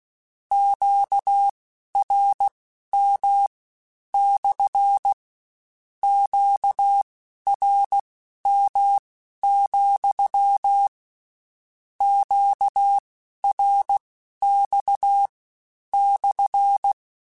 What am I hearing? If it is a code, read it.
Morse code, heard as a keyed tone, about 16 wpm, spaced as for 11 wpm, 800 Hz. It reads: QRM / QRM, QRX /
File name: Q Calls / Q Calls-mp3s / Q Calls SPSL2 @15